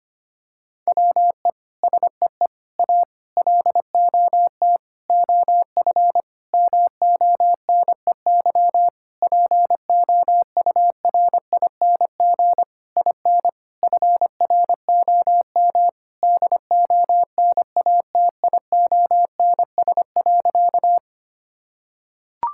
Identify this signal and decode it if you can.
Morse code, heard as a keyed tone, about 25 wpm, 700 Hz. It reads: WE SEE A LOT OF MONEY POURING IN FROM DONATIONS.
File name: practice / 25wpm / Sentences with 9-Word Phrases - 25wpm